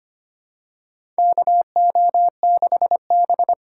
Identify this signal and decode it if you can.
Morse code, heard as a keyed tone, about 25 wpm, 700 Hz. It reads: KO6B
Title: USA Call Signs - 4x Character Spacing - 25wpm